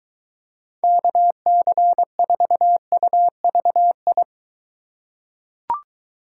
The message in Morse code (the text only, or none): KC4UVI